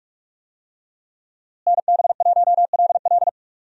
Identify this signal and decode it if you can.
Morse code, heard as a keyed tone, about 47 wpm, 700 Hz. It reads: NB1LL
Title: USA Call Signs - 5 Characters - Repeated 3x - 45wpm